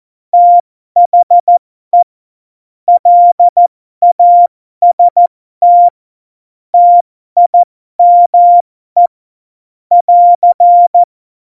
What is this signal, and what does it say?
Morse code, heard as a keyed tone, about 14 wpm, spaced as for 12 wpm, 700 Hz. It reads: THE LAST TIME <AR>
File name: Morsekurs / Witze